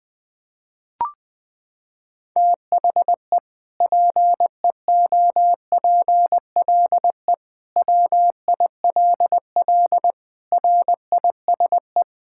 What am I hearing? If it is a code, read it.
Morse code, heard as a keyed tone, about 19 wpm, 700 Hz. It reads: THE PEOPLE WILL RISE
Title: Sentences from Top 1000 Words - Review Part 6 - 20wpm